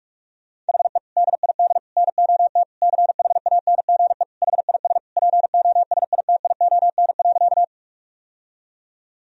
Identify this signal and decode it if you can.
Morse code, heard as a keyed tone, about 46 wpm, 700 Hz. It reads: HE DID NOT CHANGE HIS POSITION.
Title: Sentences from Top 1000 Words - Review Part 1 - 45wpm